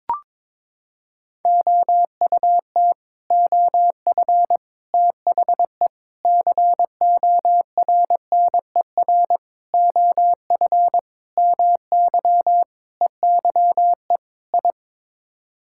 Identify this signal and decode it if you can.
Morse code, heard as a keyed tone, about 22 wpm, 700 Hz. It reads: OUT OF THE CORNER OF MY EYE I
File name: practice / 22wpm / Sets of 8 Words - 22wpm